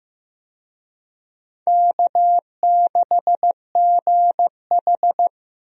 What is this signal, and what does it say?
Morse code, heard as a keyed tone, about 15 wpm, 700 Hz. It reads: K6GH
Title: USA Call Signs - 4 Characters - 15wpm